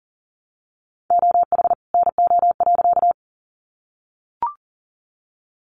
Morse code, heard as a keyed tone, about 40 wpm, 700 Hz. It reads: OH NO.
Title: Additional Easy Phrases 40wpm